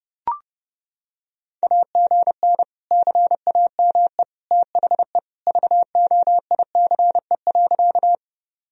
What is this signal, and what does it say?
Morse code, heard as a keyed tone, about 30 wpm, 700 Hz. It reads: AGN CAME THE VOICE.
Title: Sentences from Top 500 Words and Abbreviations 30wpm